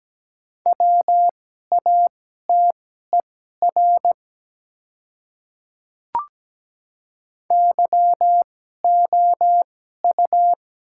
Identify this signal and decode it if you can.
Morse code, heard as a keyed tone, about 17 wpm, spaced as for 12 wpm, 700 Hz. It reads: WATER YOU
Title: 2x Character Spacing - Top 200 Words - 17wpm